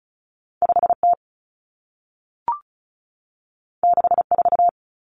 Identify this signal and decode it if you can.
Morse code, heard as a keyed tone, about 36 wpm, 700 Hz. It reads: 5T 64
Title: Common Cut Numbers - 2-Digit 35wpm